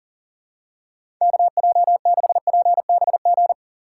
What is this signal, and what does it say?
Morse code, heard as a keyed tone, about 40 wpm, 700 Hz. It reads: KJ6PBG
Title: USA Call Signs - 3x Character Spacing - 40wpm